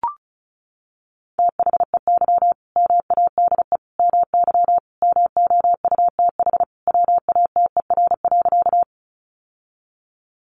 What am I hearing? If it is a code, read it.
Morse code, heard as a keyed tone, about 34 wpm, 700 Hz. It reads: THEY MADE MY MOUTH WATER.
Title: Sentences from Top 1500 Words 35wpm